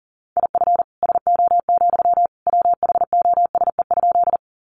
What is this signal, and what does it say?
Morse code, heard as a keyed tone, about 40 wpm, 700 Hz. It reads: IF SO, WHOSE?
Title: Sentences from Top 1200 Words 40wpm